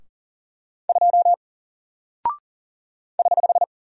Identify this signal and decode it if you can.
Morse code, heard as a keyed tone, about 40 wpm, 700 Hz. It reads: 2 <HH>
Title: Single Character (Letter:Number:Punctuation:Prosign) Alt 40wpm